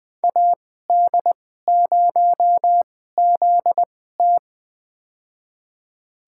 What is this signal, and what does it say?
Morse code, heard as a keyed tone, about 20 wpm, spaced as for 14 wpm, 700 Hz. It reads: AD0ZT
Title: USA Call Signs - 5 Characters - 2x Character Spacing - 20wpm